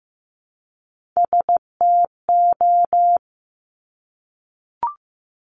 Morse code, heard as a keyed tone, about 15 wpm, 700 Hz. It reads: STO
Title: Top Three Letter Combinations in Words 15wpm